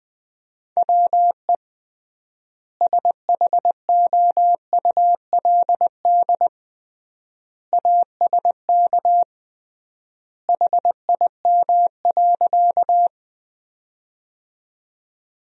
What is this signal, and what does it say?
Morse code, heard as a keyed tone, about 21 wpm, 700 Hz. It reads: WE SHOULD ASK HIM.
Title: 3x Word-Spacing - Sentences from Top 300 Words - 20wpm